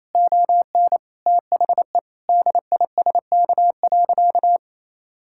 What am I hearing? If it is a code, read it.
Morse code, heard as a keyed tone, about 28 wpm, 700 Hz. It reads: ON THE DISK.
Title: Sentences from Top 1000 Words - Review Part 6 - 28wpm